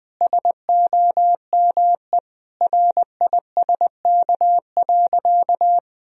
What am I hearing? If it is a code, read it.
Morse code, heard as a keyed tone, about 20 wpm, 700 Hz. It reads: SOME RISK.
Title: Sentences from Top 1000 Words - Review Part 4 - 20wpm